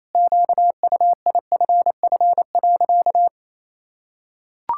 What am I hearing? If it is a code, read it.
Morse code, heard as a keyed tone, about 28 wpm, 700 Hz. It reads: QUIFF.
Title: Pangram Challenge v2 - 28wpm